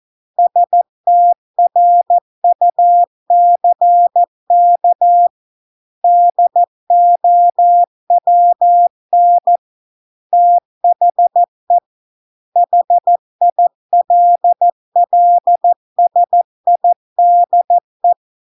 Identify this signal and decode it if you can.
Morse code, heard as a keyed tone, about 14 wpm, 700 Hz. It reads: STRUCK DOWN THE HILLSIDE